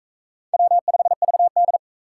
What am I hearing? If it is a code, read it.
Morse code, heard as a keyed tone, about 38 wpm, 700 Hz. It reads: W5VD